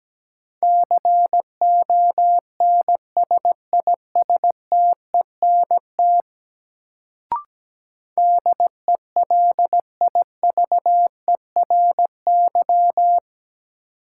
Morse code, heard as a keyed tone, about 17 wpm, 700 Hz. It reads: CONSISTENT DELIVERY
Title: Sentences from Top 1500 Words - New Word Drill - 17wpm